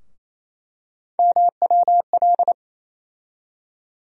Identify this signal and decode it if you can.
Morse code, heard as a keyed tone, about 28 wpm, 700 Hz. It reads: MWL
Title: USA Call Sign Postfixes - 3 Letters 28wpm